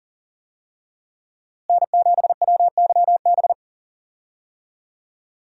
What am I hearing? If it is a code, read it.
Morse code, heard as a keyed tone, about 40 wpm, 700 Hz. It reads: N7WYB
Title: USA Call Signs - 5 Characters - Repeated 3x - 40wpm